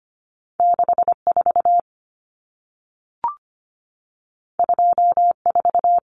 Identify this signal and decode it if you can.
Morse code, heard as a keyed tone, about 25 wpm, 700 Hz. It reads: 64 24
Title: Numbers - 2-Digit 25wpm